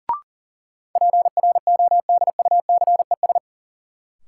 Morse code, heard as a keyed tone, about 38 wpm, 700 Hz. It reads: PRODUCES
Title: Sentences from Top 1000 Words - Review Part 6 - New Word Drill - 40wpm